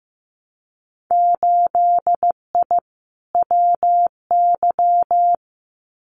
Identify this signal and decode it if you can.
Morse code, heard as a keyed tone, about 15 wpm, 700 Hz. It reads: 8I WY
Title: Winter Field Day Exchanges Encore 15wpm